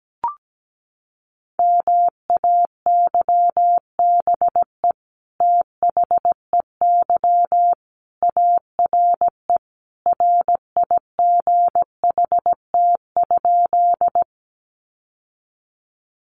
Morse code, heard as a keyed tone, about 17 wpm, 700 Hz. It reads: MAYBE THEY ARE RIGHT?
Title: Sentences from Top 1000 Words 17wpm